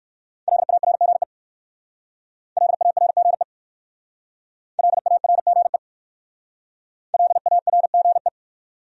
Morse code, heard as a keyed tone, about 57 wpm, 700 Hz. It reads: LARGE LARGE LARGE LARGE